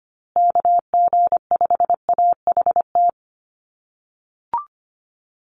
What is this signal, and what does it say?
Morse code, heard as a keyed tone, about 25 wpm, 700 Hz. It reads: KG5AHT
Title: USA Call Signs Repeated 3x 25wpm